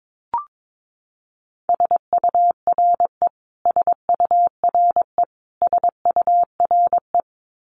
Morse code, heard as a keyed tone, about 22 wpm, 700 Hz. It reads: SURE SURE SURE
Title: Top 300 Words Repeated 22wpm